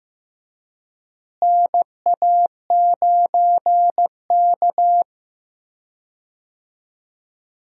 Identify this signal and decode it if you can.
Morse code, heard as a keyed tone, about 15 wpm, 700 Hz. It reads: NA9K
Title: USA Call Signs - 4 Characters - Repeated 2x - 15wpm